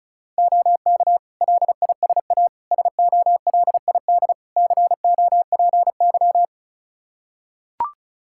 Morse code, heard as a keyed tone, about 34 wpm, 700 Hz. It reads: OK LISA SOLID COPY
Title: Practice QSO - Exchange 3 - More Info - 35wpm